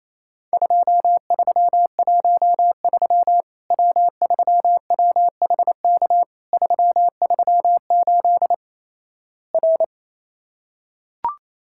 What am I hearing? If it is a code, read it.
Morse code, heard as a keyed tone, about 28 wpm, 700 Hz. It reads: 2313 W3WHK 338 R
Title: Worked All Europe DX Contest - QTC Exchange - 28wpm